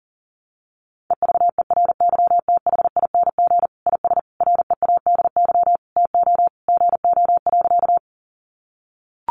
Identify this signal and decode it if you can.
Morse code, heard as a keyed tone, about 37 wpm, 700 Hz. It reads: EVERYTHING IS READY TO GO.
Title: Sentences from Top 700 Words 40wpm